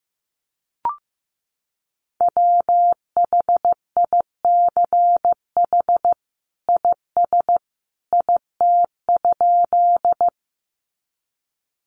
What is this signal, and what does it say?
Morse code, heard as a keyed tone, about 15 wpm, 700 Hz. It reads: WHICH IS IT?
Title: Sentences from Top 100 Words 15wpm